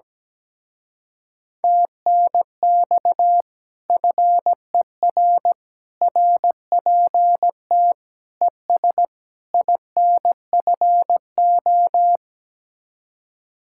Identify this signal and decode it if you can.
Morse code, heard as a keyed tone, about 17 wpm, 700 Hz. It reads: TNX FER RPT ES INFO
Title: Speed-Racing Common QSO Phrases - 17wpm